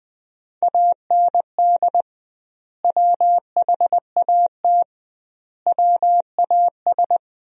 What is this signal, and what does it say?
Morse code, heard as a keyed tone, about 21 wpm, 700 Hz. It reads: AND WHAT WAS
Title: 2x Word-Spacing - Sentences from Top 100 Words - 20wpm